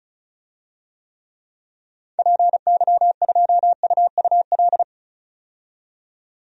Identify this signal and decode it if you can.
Morse code, heard as a keyed tone, about 34 wpm, 700 Hz. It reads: PY2UUL
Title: DXCC-100 Call Signs - 6 Characters - 35wpm